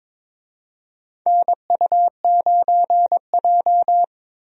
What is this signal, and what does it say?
Morse code, heard as a keyed tone, about 22 wpm, 700 Hz. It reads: NU9J
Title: USA Call Signs - 4 Characters - 22wpm